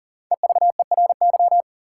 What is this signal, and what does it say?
Morse code, heard as a keyed tone, about 40 wpm, 700 Hz. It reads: EVERY
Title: Sets of 5 Words - New Word Drill - 40wpm